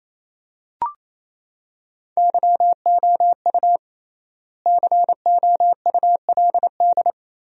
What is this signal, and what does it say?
Morse code, heard as a keyed tone, about 28 wpm, 700 Hz. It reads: YOU COULD
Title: 3x Word-Spacing - Sentences from Top 500 Words - 28wpm